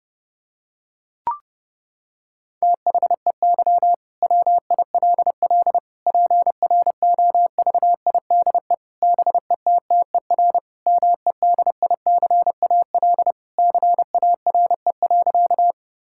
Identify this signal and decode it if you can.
Morse code, heard as a keyed tone, about 32 wpm, 700 Hz. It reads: THEY WILL PROVIDE BETTER MEDICAL CARE.
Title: Sentences from Top 1000 Words - Review Part 5 - 30wpm